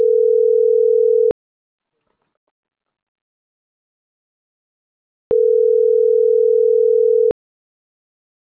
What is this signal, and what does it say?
Ringback tone (North America), 440 + 480 Hz.
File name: CENTRAL MAERSK / out nogueronmsk